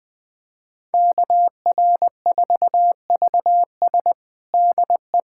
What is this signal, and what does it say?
Morse code, heard as a keyed tone, about 21 wpm, 700 Hz. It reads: KR4VS DE
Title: USA Callsign DE Callsign - 5 Characters - 20wpm